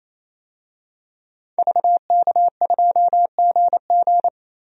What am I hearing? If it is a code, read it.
Morse code, heard as a keyed tone, about 28 wpm, 700 Hz. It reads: VK2GG
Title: DXCC-100 Call Signs - 5 Characters - 28wpm